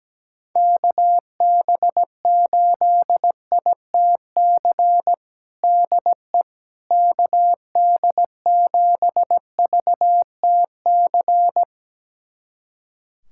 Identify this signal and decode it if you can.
Morse code, heard as a keyed tone, about 17 wpm, 700 Hz. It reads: KB8ITC DE KD7VTC
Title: USA Callsign DE Callsign - 6 Characters - 17wpm